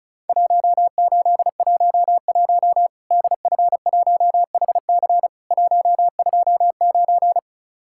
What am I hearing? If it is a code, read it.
Morse code, heard as a keyed tone, about 34 wpm, 700 Hz. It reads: 1811 DF1HC 129
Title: Worked All Europe DX Contest - QTC Exchange - 35wpm